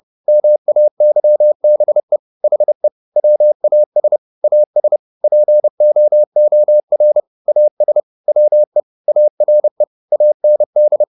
Morse code, heard as a keyed tone, about 30 wpm, 600 Hz. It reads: MAYBE HE WAS AS POOR AS WE ARE AND
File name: the old man and the sea / the old man and the sea 1 30wpm